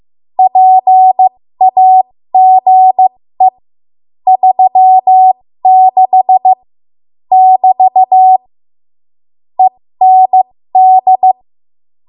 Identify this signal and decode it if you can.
Morse code, heard as a keyed tone, about 15 wpm, spaced as for 13 wpm, 750 Hz. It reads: PAGE 36 <BT> END
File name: Morse / 13WPM